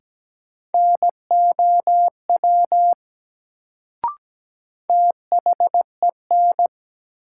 Morse code, heard as a keyed tone, about 17 wpm, 700 Hz. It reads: NOW THEN
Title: Sets of 2 Words - New Word Drill - 17wpm